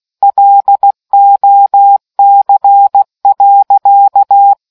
Morse code, heard as a keyed tone, about 16 wpm, 800 Hz. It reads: LOC.